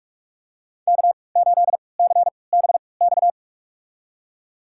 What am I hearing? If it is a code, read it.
Morse code, heard as a keyed tone, about 43 wpm, spaced as for 25 wpm, 700 Hz. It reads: K8CBX
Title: USA Call Signs - 5 Characters - 3x Character Spacing - 45wpm